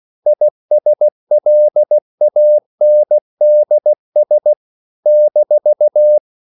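Morse code, heard as a keyed tone, about 16 wpm, 600 Hz. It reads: ISLANDS -
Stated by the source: CW practice of national capitals at 16 words per minute, from ON6ZQ.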